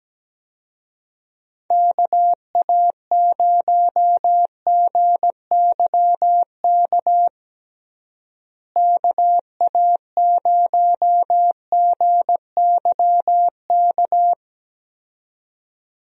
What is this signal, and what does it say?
Morse code, heard as a keyed tone, about 17 wpm, 700 Hz. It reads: KA0GYK KA0GYK
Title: USA Call Signs - 6 Characters - Repeated 2x - 17wpm